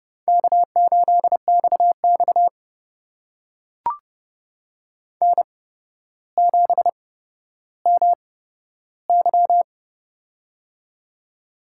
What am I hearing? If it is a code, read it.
Morse code, heard as a keyed tone, about 30 wpm, spaced as for 13 wpm, 700 Hz. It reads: K8XX N7MY
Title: USA Call Signs - 4 Characters - 8x Character Spacing - 30wpm